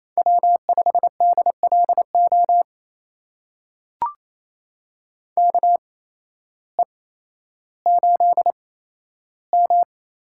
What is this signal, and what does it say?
Morse code, heard as a keyed tone, about 28 wpm, spaced as for 13 wpm, 700 Hz. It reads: W5DLO KE8M
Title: USA Call Signs - 8x Character Spacing - 28wpm